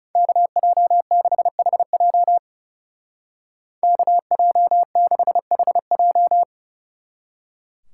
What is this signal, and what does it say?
Morse code, heard as a keyed tone, about 36 wpm, 700 Hz. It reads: KJ6HJ KJ6HJ
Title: Speed Racing - USA 2x2 Call Signs 30wpm